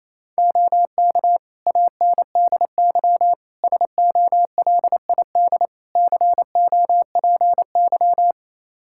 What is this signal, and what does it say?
Morse code, heard as a keyed tone, about 28 wpm, 700 Hz. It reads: OK ANDY SOLID COPY
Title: K1USN SST QSO Practice 28wpm